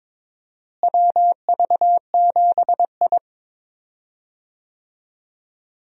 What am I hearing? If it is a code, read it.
Morse code, heard as a keyed tone, about 22 wpm, 700 Hz. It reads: WV7I
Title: USA Call Signs - 4 Characters - Repeated 3x - 22wpm